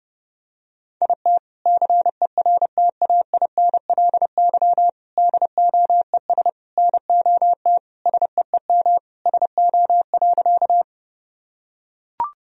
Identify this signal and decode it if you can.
Morse code, heard as a keyed tone, about 30 wpm, 700 Hz. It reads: IT CERTAINLY DOES NOT SEEM SO.
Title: Sentences from Top 1000 Words - Review Part 1 - 30wpm